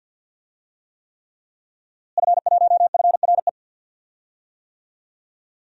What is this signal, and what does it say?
Morse code, heard as a keyed tone, about 51 wpm, 700 Hz. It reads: F1FRE
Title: DXCC-100 Call Signs - 5 Characters - Repeated 3x - 50wpm